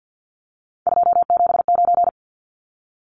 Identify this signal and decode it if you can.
Morse code, heard as a keyed tone, about 50 wpm, 700 Hz. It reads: 279
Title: Numbers - 3-Digit 50wpm